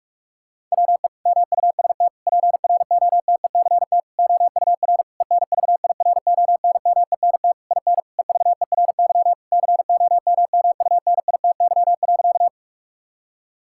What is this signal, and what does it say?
Morse code, heard as a keyed tone, about 46 wpm, 700 Hz. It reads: WE MUST PROTECT OUR ENVIRONMENT IN EVERY COMMUNITY.
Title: Sentences from Top 1000 Words - Review Part 5 - 45wpm